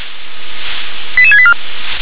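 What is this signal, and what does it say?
Five-tone selective call (ZVEI), heard as quick runs of five tones, tones of about 70 ms. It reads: 88684
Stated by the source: ZVEI five-tone selective call 88737, received via SDR.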